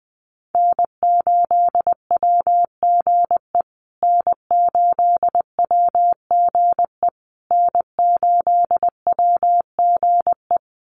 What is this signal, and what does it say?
Morse code, heard as a keyed tone, about 20 wpm, 700 Hz. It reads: N8WGE N8WGE N8WGE
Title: USA Call Signs Repeated 3x 20wpm